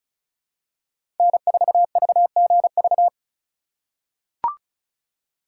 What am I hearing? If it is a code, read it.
Morse code, heard as a keyed tone, about 34 wpm, 700 Hz. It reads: N4VGV
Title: USA Call Signs - 5 Characters - 2x Character Spacing - 35wpm